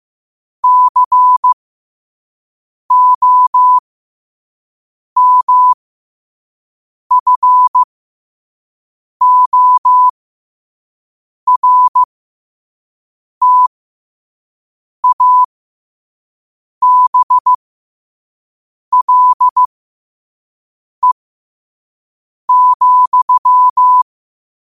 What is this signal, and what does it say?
Morse code, heard as a keyed tone, about 15 wpm, spaced as for 5 wpm, 1000 Hz. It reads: COMFORTABLE,
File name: qotdfiles / 5wpm